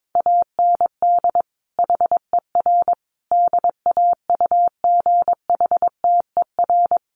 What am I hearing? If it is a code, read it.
Morse code, heard as a keyed tone, about 22 wpm, 700 Hz. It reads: AND HER DAUGHTER
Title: Sentences from Top 1500 Words 22wpm